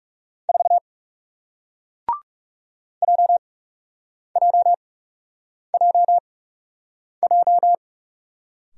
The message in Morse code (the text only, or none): 4 JJJJ